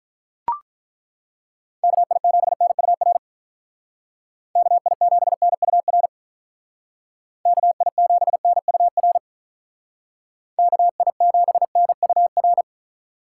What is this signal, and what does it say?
Morse code, heard as a keyed tone, about 42 wpm, 700 Hz. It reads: KI7NUR KI7NUR KI7NUR KI7NUR
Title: Speed Racing - USA 2x3 Call Signs 35wpm